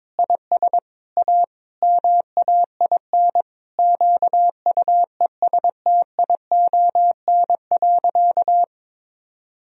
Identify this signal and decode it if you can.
Morse code, heard as a keyed tone, about 22 wpm, 700 Hz. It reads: IS A MAIN QUESTION.
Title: Sentences from Top 1000 Words - Review Part 2 - 22wpm